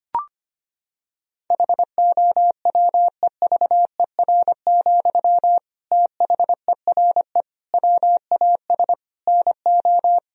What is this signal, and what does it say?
Morse code, heard as a keyed tone, about 25 wpm, 700 Hz. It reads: HOWEVER, THERE WAS NO
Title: Sentences from Top 1000 Words - Review Part 4 - 25wpm